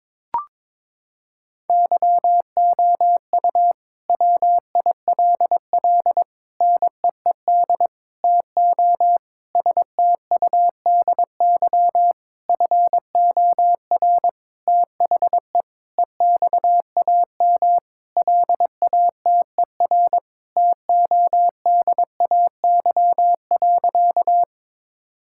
Morse code, heard as a keyed tone, about 22 wpm, 700 Hz. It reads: YOU WILL NEED TO STUDY FOR THE EXAM LATER TODAY.
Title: Sentences from Top 1000 Words - Review Part 5 - 22wpm